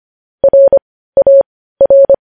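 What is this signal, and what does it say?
Morse code, heard as a keyed tone, about 25 wpm, spaced as for 15 wpm, 550 Hz. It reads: RAR
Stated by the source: News headlines sent as Morse practice at 15.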